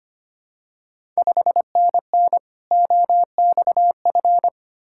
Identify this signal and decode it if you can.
Morse code, heard as a keyed tone, about 25 wpm, 690 Hz. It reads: 5NN OXF
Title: Ontario QSO Party Exchange 25wpm